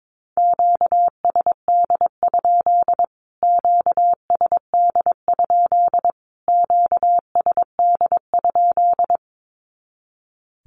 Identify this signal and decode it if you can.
Morse code, heard as a keyed tone, about 22 wpm, 700 Hz. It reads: QSD? QSD? QSD?
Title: Q-Codes Repeated 3x 22wpm